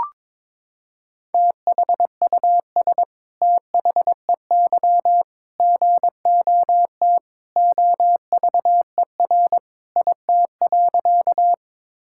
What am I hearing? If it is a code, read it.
Morse code, heard as a keyed tone, about 22 wpm, 700 Hz. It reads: THUS THEY GOT OVER IT.
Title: Sentences from Top 1000 Words - Review Part 5 - 22wpm